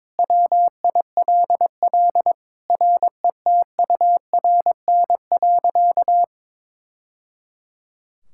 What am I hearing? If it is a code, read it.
Morse code, heard as a keyed tone, about 22 wpm, 700 Hz. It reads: WILL RETURN.
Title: Sentences from Top 1000 Words - Review Part 6 - 22wpm